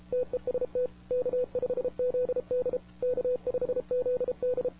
Morse code, heard as a keyed tone, about 33 wpm, 500 Hz. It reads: TEST K5ZD K5ZD
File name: wwcw08 / k5zd wwcw08 1700z